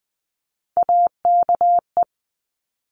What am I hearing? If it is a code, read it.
Morse code, heard as a keyed tone, about 19 wpm, 700 Hz. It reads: AKE
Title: Top Three Letter Combinations in Words 20wpm